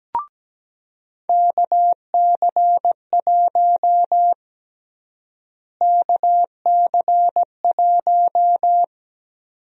Morse code, heard as a keyed tone, about 17 wpm, 700 Hz. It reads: KC1 KC1
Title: USA Call Sign Prefixes - Repeated 2x - 17wpm